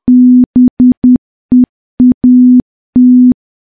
Morse code, heard as a keyed tone, about 10 wpm, 250 Hz. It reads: BEAT
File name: morse / morse-training-10WPM